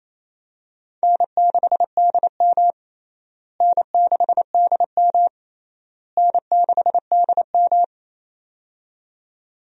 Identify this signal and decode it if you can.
Morse code, heard as a keyed tone, about 28 wpm, 700 Hz. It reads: N6DM N6DM N6DM
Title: USA Call Signs - 4 Characters - Repeated 3x - 28wpm